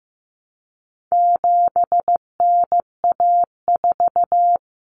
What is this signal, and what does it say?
Morse code, heard as a keyed tone, about 15 wpm, 700 Hz. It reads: 7NA4
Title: Common Cut Numbers - 4-Digit 15wpm